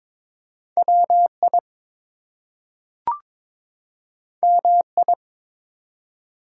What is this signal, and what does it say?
Morse code, heard as a keyed tone, about 22 wpm, 700 Hz. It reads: WI MI
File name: ARRL Sections v2 - 22wpm